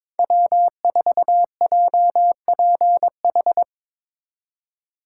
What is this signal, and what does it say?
Morse code, heard as a keyed tone, about 22 wpm, 700 Hz. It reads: W4JPH
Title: K1USN SST QSO Practice - 22wpm